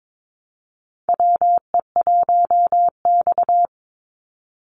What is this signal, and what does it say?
Morse code, heard as a keyed tone, about 22 wpm, 700 Hz. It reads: WE1X
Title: USA Call Signs 22wpm